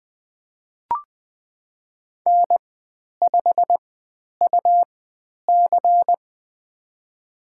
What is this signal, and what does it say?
Morse code, heard as a keyed tone, about 20 wpm, spaced as for 10 wpm, 700 Hz. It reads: N5UC
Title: 102 Lesson 17g - Letter u - Call Signs 20wpm